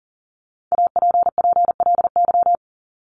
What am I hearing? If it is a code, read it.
Morse code, heard as a keyed tone, about 40 wpm, 700 Hz. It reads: APPLY
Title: ED Suffix - Top 100 Words with Base Words 40wpm